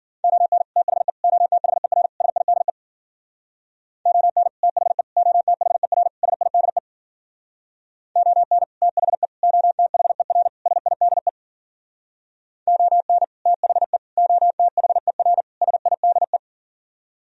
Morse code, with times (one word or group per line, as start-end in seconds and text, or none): 0.2-0.6 ON
0.8-1.1 THE
1.2-2.1 OTHER
2.2-2.7 SIDE
4.1-4.5 ON
4.6-5.0 THE
5.2-6.1 OTHER
6.2-6.8 SIDE
8.2-8.6 ON
8.8-9.3 THE
9.4-10.5 OTHER
10.7-11.3 SIDE
12.7-13.2 ON
13.5-14.0 THE
14.2-15.4 OTHER
15.6-16.4 SIDE